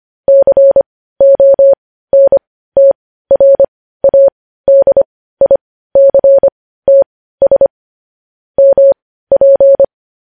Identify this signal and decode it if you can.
Morse code, heard as a keyed tone, about 25 wpm, spaced as for 15 wpm, 550 Hz. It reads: CONTRADICTS MP
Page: News headlines in Morse code at 15 wpm